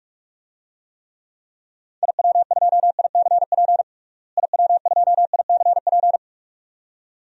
Morse code, heard as a keyed tone, about 46 wpm, 700 Hz. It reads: IW2ICP IW2ICP
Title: DXCC-100 Call Signs - 6 Characters - Repeated 2x - 45wpm